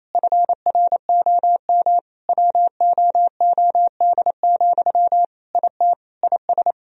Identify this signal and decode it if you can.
Morse code, heard as a keyed tone, about 28 wpm, 700 Hz. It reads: FROM WOOD, IT IS
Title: Sentences from Top 1000 Words - Review Part 3 - 28wpm